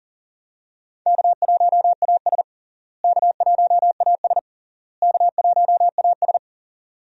Morse code, heard as a keyed tone, about 40 wpm, 700 Hz. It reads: K1AS K1AS K1AS
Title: USA Call Signs - 4 Characters - Repeated 3x - 40wpm